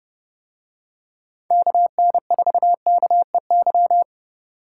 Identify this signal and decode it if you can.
Morse code, heard as a keyed tone, about 28 wpm, 700 Hz. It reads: KN4KEY